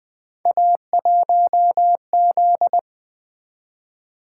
Morse code, heard as a keyed tone, about 20 wpm, 700 Hz. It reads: A1Z
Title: Three Letter-Number Combinations - 20wpm